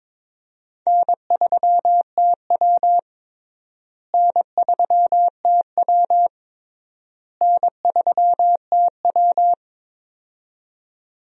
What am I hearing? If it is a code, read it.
Morse code, heard as a keyed tone, about 21 wpm, 700 Hz. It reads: N3TW N3TW N3TW
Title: USA Call Signs - 4 Characters - Repeated 3x - 22wpm